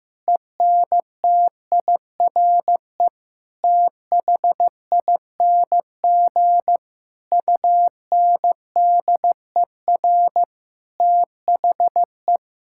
Morse code, heard as a keyed tone, about 15 wpm, 700 Hz. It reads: ENTIRE THING UNDER THE
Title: Sentences from Top 1000 Words - Review Part 5 - 15wpm